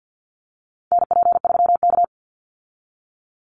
Morse code, heard as a keyed tone, about 50 wpm, 700 Hz. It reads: NP3K
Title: Call Signs Repeated 3x - 50wpm